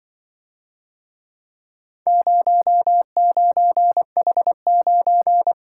Morse code, heard as a keyed tone, about 24 wpm, 700 Hz. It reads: 09H9
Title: Speed-Racing - Four Letter-Number Combinations - 20wpm